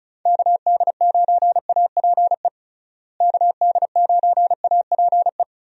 Morse code, heard as a keyed tone, about 34 wpm, 700 Hz. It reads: KD9APE KD9APE
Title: USA Call Signs - 6 Characters - Repeated 2x - 35wpm